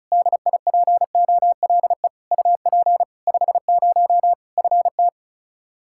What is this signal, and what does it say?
Morse code, heard as a keyed tone, about 34 wpm, 700 Hz. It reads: DIPOLE UP 50 FT
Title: Practice QSO - Exchange 2 - Info - 35wpm